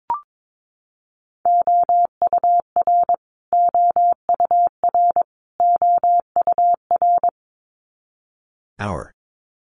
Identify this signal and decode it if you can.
Morse code, heard as a keyed tone, about 22 wpm, 700 Hz. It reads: OUR OUR OUR
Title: Top 500 Words Repeated 22wpm